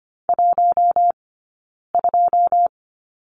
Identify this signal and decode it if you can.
Morse code, heard as a keyed tone, about 25 wpm, 700 Hz. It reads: 1 2
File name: Lesson 17d - Letter u - Two Characters 25wpm